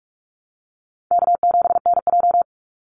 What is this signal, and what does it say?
Morse code, heard as a keyed tone, about 44 wpm, 700 Hz. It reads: K7NJ
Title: Easy Call Signs Repeated 2x - 45wpm